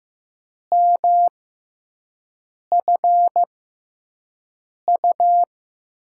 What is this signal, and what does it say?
Morse code, heard as a keyed tone, about 15 wpm, 700 Hz. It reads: M F U
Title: USA Call Signs - 6x Character Spacing - 15wpm